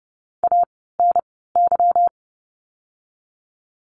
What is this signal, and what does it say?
Morse code, heard as a keyed tone, about 30 wpm, spaced as for 17 wpm, 700 Hz. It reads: ANY
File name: course / 30:10wpm / 230 Lesson 37f - Letter x - Words 30wpm